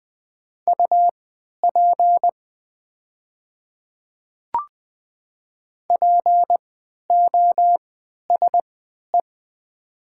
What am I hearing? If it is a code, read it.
Morse code, heard as a keyed tone, about 20 wpm, spaced as for 11 wpm, 700 Hz. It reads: UP POSE
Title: 3x Character Spacing - Top 500 Words - 20wpm